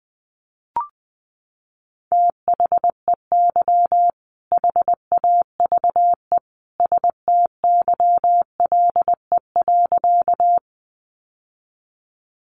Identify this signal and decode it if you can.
Morse code, heard as a keyed tone, about 20 wpm, 700 Hz. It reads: THEY HAVE STYLE.
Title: Sentences from Top 800 Words 20wpm